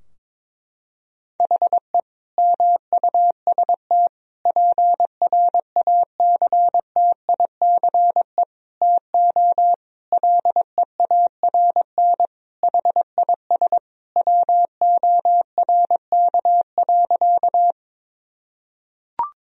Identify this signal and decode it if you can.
Morse code, heard as a keyed tone, about 21 wpm, 700 Hz. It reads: HE MUST PRACTICE TO LEARN HIS WORK.
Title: Sentences from Top 1000 Words - Review Part 4 - 22wpm